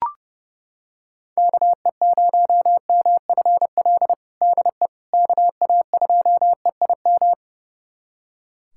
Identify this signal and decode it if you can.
Morse code, heard as a keyed tone, about 32 wpm, 700 Hz. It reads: KE0MFL DE KA2EIM
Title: USA Callsign DE Callsign - 6 Characters - 30wpm